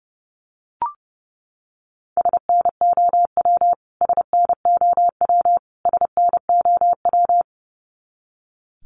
Morse code, heard as a keyed tone, about 30 wpm, 700 Hz. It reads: SNOW SNOW SNOW
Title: Top 500 Words Repeated 30wpm